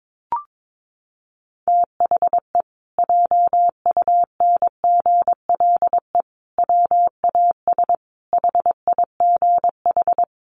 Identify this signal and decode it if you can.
Morse code, heard as a keyed tone, about 22 wpm, 700 Hz. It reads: THE JUNGLE WAS HIGH
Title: A Sound Of Thunder 22wpm